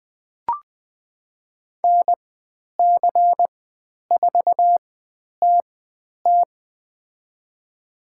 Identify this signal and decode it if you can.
Morse code, heard as a keyed tone, about 20 wpm, spaced as for 10 wpm, 700 Hz. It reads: NC4TT
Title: 095 Lesson 16g - Letter c - Call Signs 20wpm